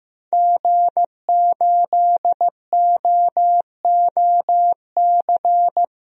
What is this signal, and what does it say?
Morse code, heard as a keyed tone, about 15 wpm, 700 Hz. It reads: G8OOC
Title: DXCC-100 Call Signs - 5 Characters - Repeated 3x - 15wpm